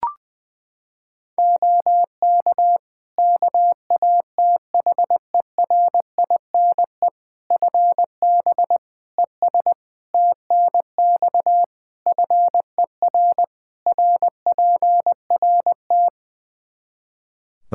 Morse code, heard as a keyed tone, about 20 wpm, 700 Hz. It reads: OK KATHERINE FB ES TNX FER RPRT
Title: K1USN SST QSO Practice 20wpm